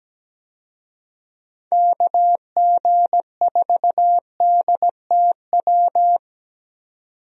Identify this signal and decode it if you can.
Morse code, heard as a keyed tone, about 17 wpm, 700 Hz. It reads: KG4DTW
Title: USA Call Signs - 6 Characters - 17wpm